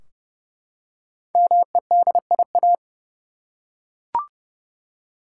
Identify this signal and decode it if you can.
Morse code, heard as a keyed tone, about 32 wpm, 700 Hz. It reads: MEDIA
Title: Five Letter Words - 30wpm